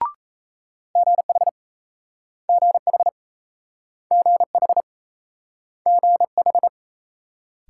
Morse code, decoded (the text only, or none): GH GH GH GH